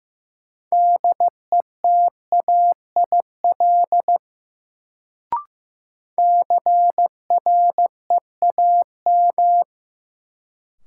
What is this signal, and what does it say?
Morse code, heard as a keyed tone, about 16 wpm, 700 Hz. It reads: DETAIL CREAM
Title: Sentences from Top 1500 Words - New Word Drill Alt - 15wpm